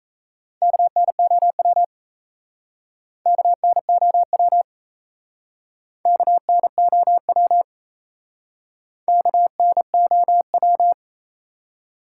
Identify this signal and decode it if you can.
Morse code, heard as a keyed tone, about 33 wpm, 700 Hz. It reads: KNOW KNOW KNOW KNOW
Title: Speed-Racing Top 180 QSO Elements 28wpm